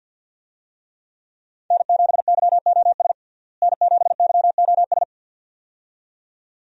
Morse code, heard as a keyed tone, about 51 wpm, 700 Hz. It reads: N7YOS N7YOS
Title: USA Call Signs - 5 Characters - Repeated 2x - 50wpm